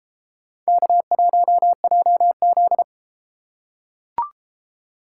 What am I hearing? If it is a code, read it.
Morse code, heard as a keyed tone, about 31 wpm, 700 Hz. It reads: K1JZ